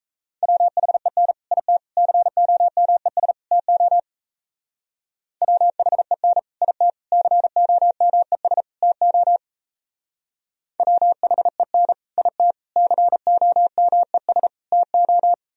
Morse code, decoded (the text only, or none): WHEN IT COMES TO WHEN IT COMES TO WHEN IT COMES TO